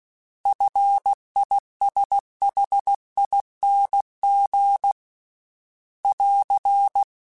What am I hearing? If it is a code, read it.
Morse code, heard as a keyed tone, about 16 wpm, 800 Hz. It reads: FISHING <AR>
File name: Single F mp3s / Single F Prefixes 3x2 Twice @15